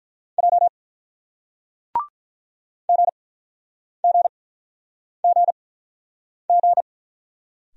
Morse code, heard as a keyed tone, about 46 wpm, spaced as for 9 wpm, 700 Hz. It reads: J GGGG